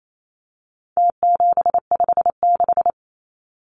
Morse code, heard as a keyed tone, about 28 wpm, 700 Hz. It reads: T756
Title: Numbers - Contest Style - 4-Digit 28wpm